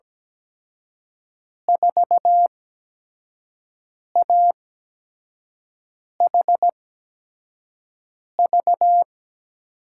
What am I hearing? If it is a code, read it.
Morse code, heard as a keyed tone, about 17 wpm, spaced as for 5 wpm, 700 Hz. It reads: 4AHV
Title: USA Call Signs - 6 Characters - 8x Character Spacing - 17wpm